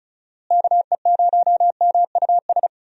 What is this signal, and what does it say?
Morse code, heard as a keyed tone, about 34 wpm, 700 Hz. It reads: KE0MUS
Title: USA Call Signs - 6 Characters - Repeated 3x - 35wpm